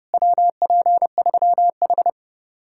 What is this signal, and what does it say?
Morse code, heard as a keyed tone, about 38 wpm, 700 Hz. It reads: WP3H